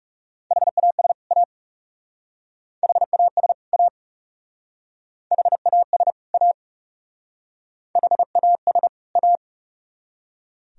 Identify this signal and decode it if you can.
Morse code, heard as a keyed tone, about 35 wpm, 700 Hz. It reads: HAS A HAS A HAS A HAS A
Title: Speed-Racing - Sets of 2 Words 30wpm